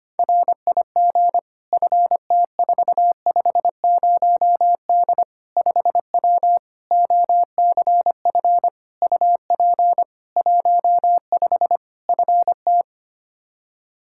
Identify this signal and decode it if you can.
Morse code, heard as a keyed tone, about 25 wpm, 700 Hz. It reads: RIG FT450D 5W OCF UP 15 FT
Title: K1USN SST QSO Practice 25wpm